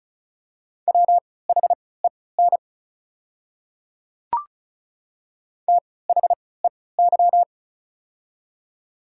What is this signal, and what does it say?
Morse code, heard as a keyed tone, about 36 wpm, spaced as for 20 wpm, 690 Hz. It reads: WHEN THEY
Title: 3x Character Spacing - Top 100 Words - 35wpm